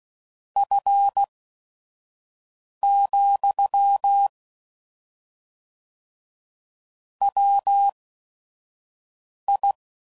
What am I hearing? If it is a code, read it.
Morse code, heard as a keyed tone, about 16 wpm, spaced as for 5 wpm, 800 Hz. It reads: F, WI
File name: CW Code Groups @5WPM / Weather Sets / Weather 5 22